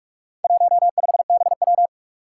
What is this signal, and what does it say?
Morse code, heard as a keyed tone, about 46 wpm, 700 Hz. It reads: 15BW
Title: Four Challenging Letter-Number Combinations 45wpm